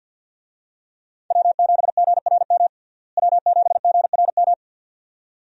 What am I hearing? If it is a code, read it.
Morse code, heard as a keyed tone, about 48 wpm, 700 Hz. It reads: W7GRM W7GRM
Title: USA Call Signs - 5 Characters - Repeated 2x - 50wpm